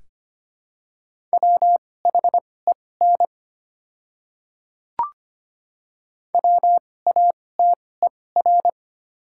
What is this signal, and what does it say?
Morse code, heard as a keyed tone, about 25 wpm, spaced as for 18 wpm, 700 Hz. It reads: WHEN WATER
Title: 2x Character Spacing - Top 100 Words - 25wpm